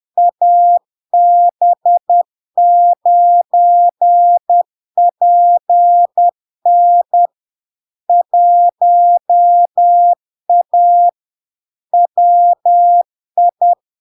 Morse code, heard as a keyed tone, about 10 wpm, 700 Hz. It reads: AB9PN 1A WI
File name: LARK-FD-2022-Log-in-CW-10wpm-700-Hz